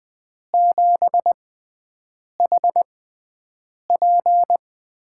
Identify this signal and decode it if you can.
Morse code, heard as a keyed tone, about 20 wpm, spaced as for 7 wpm, 700 Hz. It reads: 7HP
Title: USA Call Signs - 4 Characters - 6x Character Spacing - 20wpm